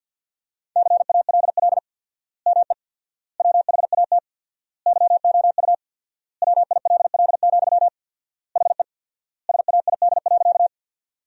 Morse code, heard as a keyed tone, about 50 wpm, 700 Hz. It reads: CALL ME WHAT YOU WILL, HE SAID.
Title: 4x Word-Spacing - Sentences from Top 200 Words - 50wpm